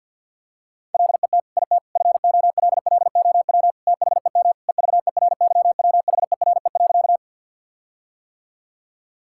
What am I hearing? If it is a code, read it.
Morse code, heard as a keyed tone, about 50 wpm, 700 Hz. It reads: LET IT FOLLOW THEM EVERYWHERE.
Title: Sentences from Top 1600 Words 50wpm